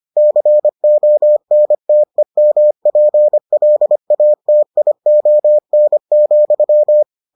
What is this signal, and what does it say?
Morse code, heard as a keyed tone, about 25 wpm, 600 Hz. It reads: CONTEMPLATION,